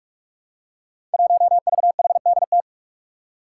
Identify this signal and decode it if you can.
Morse code, heard as a keyed tone, about 42 wpm, 700 Hz. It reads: 1VHDT
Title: Five Challenging Letter-Number Combinations 45wpm